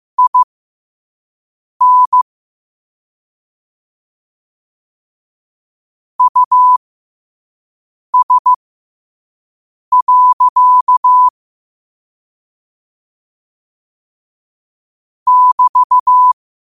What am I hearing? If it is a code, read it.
Morse code, heard as a keyed tone, about 15 wpm, spaced as for 5 wpm, 1000 Hz. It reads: IN US. <BT>
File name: qotdfiles / 5wpm